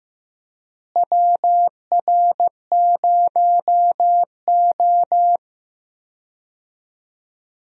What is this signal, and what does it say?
Morse code, heard as a keyed tone, about 15 wpm, 700 Hz. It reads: WR0O
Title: USA Call Signs - 4 Characters - Repeated 3x - 15wpm